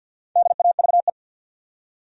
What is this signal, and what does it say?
Morse code, heard as a keyed tone, about 53 wpm, 700 Hz. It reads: DAVE
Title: USA Common Names - 50wpm